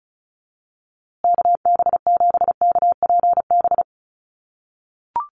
Morse code, heard as a keyed tone, about 34 wpm, 700 Hz. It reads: KB7KPB
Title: USA Call Signs Repeated 3x 35wpm